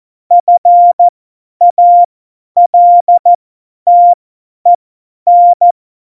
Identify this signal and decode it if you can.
Morse code, heard as a keyed tone, about 14 wpm, spaced as for 10 wpm, 700 Hz. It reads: FALTEN
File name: Morsekurs / Witze / Witz 26. November 2019, 50 F70